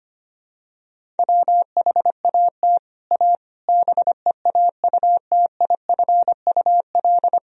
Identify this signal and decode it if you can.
Morse code, heard as a keyed tone, about 25 wpm, 700 Hz. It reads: WHAT A BEAUTIFUL
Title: Sentences from Top 1000 Words - Review Part 1 - 25wpm